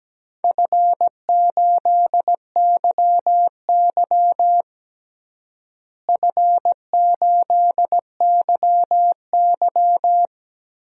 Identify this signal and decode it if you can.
Morse code, heard as a keyed tone, about 17 wpm, 700 Hz. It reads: F8YY F8YY
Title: DXCC-100 Call Signs - 4 Characters - Repeated 3x - 17wpm